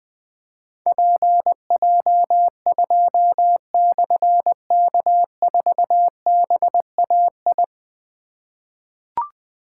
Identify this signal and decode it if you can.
Morse code, heard as a keyed tone, about 20 wpm, 700 Hz. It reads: PJ2/K4BAI
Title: Difficult Call Signs 20wpm